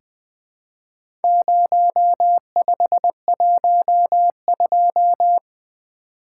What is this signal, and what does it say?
Morse code, heard as a keyed tone, about 20 wpm, 700 Hz. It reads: 0512
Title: Four Letter-Number Combinations - 20wpm